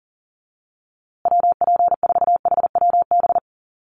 Morse code, heard as a keyed tone, about 40 wpm, 700 Hz. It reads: WP4HWB
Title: USA Call Signs Repeated 2x 40wpm